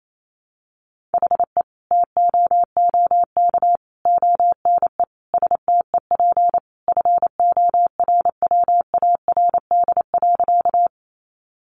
Morse code, heard as a keyed tone, about 28 wpm, 700 Hz. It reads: HE TOOK ONE STEP FORWARD.
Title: Sentences from Top 1300 Words 28wpm